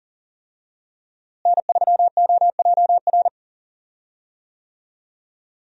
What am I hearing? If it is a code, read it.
Morse code, heard as a keyed tone, about 40 wpm, 700 Hz. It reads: N3OJR
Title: USA Call Signs - 5 Characters - Repeated 2x - 40wpm